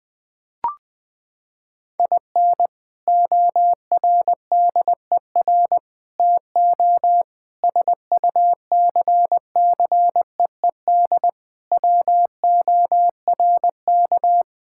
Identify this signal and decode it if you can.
Morse code, heard as a keyed tone, about 21 wpm, 700 Hz. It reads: IN ORDER TO SUCCEED WORK
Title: Sentences with 3-Word Phrases - 20wpm